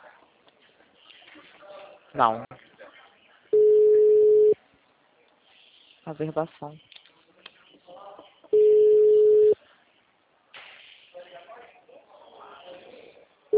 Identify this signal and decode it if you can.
Ringback tone (Europe, 425 Hz), with speech.